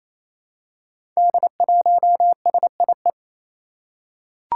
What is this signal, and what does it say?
Morse code, heard as a keyed tone, about 28 wpm, 700 Hz. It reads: D1SIE
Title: Five Challenging Letter-Number Combinations 28wpm